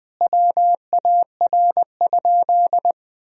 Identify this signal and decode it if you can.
Morse code, heard as a keyed tone, about 20 wpm, 700 Hz. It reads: WAR?
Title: Sentences from Top 1000 Words - Review Part 4 - 20wpm